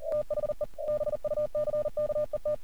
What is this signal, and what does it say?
Morse code, heard as a keyed tone, about 40 wpm, 600 Hz. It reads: THE BUCKET
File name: archived / CRSnet